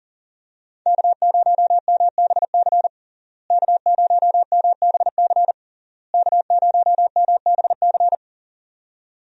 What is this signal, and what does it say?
Morse code, heard as a keyed tone, about 37 wpm, 700 Hz. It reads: K0MBC K0MBC K0MBC
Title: USA Call Signs - 5 Characters - Repeated 3x - 40wpm